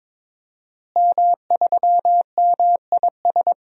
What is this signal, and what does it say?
Morse code, heard as a keyed tone, about 22 wpm, 700 Hz. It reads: M3MIS
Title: DXCC-100 Call Signs - 5 Characters - Repeated 2x - 22wpm